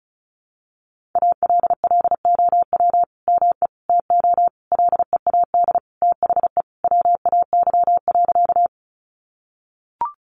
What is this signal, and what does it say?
Morse code, heard as a keyed tone, about 34 wpm, 700 Hz. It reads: ALLOW ME TO LEAD THE WAY.
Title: Sentences from Top 700 Words 35wpm